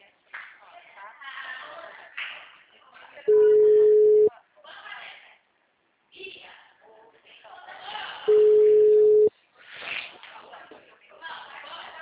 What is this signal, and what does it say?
Ringback tone (Europe, 425 Hz), with speech.
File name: out- -2015- - - .34590